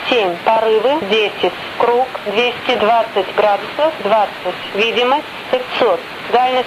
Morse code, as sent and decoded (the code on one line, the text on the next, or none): none